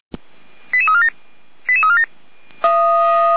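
Five-tone selective call (ZVEI), heard as quick runs of five tones, tones of about 70 ms. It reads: 88347 88347
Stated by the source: ZVEI five-tone selective call 88102 received on an SDR.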